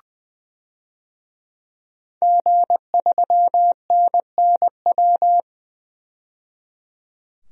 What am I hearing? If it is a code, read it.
Morse code, heard as a keyed tone, about 20 wpm, 700 Hz. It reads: G3NNW
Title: DXCC-100 Call Signs - 5 Characters - 20wpm